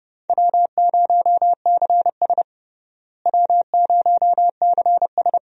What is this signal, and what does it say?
Morse code, heard as a keyed tone, about 30 wpm, 700 Hz. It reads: W0CS W0CS
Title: USA Call Signs - 4 Characters - Repeated 2x - 30wpm